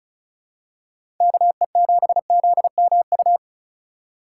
Morse code, heard as a keyed tone, about 36 wpm, 700 Hz. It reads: KE7ZMU
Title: USA Call Signs - 6 Characters - 4x Character Spacing - 35wpm